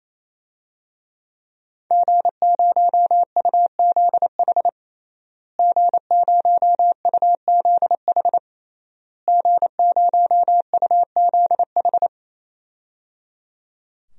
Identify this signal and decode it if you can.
Morse code, heard as a keyed tone, about 28 wpm, 700 Hz. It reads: G0UZH G0UZH G0UZH
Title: DXCC-100 Call Signs - 5 Characters - Repeated 3x - 28wpm